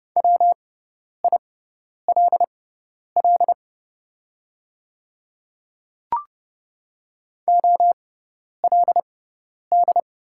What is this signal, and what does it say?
Morse code, heard as a keyed tone, about 30 wpm, spaced as for 10 wpm, 700 Hz. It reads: WILL OLD
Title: 6x Character Spacing - Top 200 Words - 30wpm